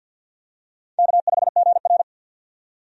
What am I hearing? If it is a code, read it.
Morse code, heard as a keyed tone, about 48 wpm, 700 Hz. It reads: K5GR
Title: USA Call Signs - 4 Characters - 50wpm